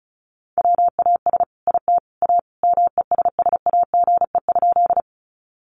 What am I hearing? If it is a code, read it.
Morse code, heard as a keyed tone, about 34 wpm, 700 Hz. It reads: WAS IT A MESSAGE?
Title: Sentences from Top 1100 Words 35wpm